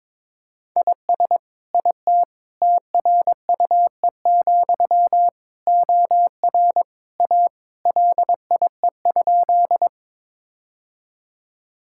Morse code, heard as a keyed tone, about 21 wpm, 700 Hz. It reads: IS IT TRUE, OR A LIE?
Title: Sentences from Top 1600 Words 22wpm